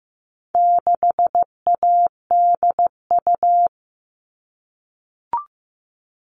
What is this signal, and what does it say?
Morse code, heard as a keyed tone, about 15 wpm, 700 Hz. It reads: 6ADU
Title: All Cut Numbers - 4-Digit 15wpm